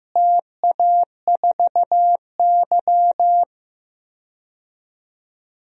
Morse code, heard as a keyed tone, about 15 wpm, 700 Hz. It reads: TA4Y
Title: DXCC-100 Call Signs - 4 Characters - Repeated 3x - 15wpm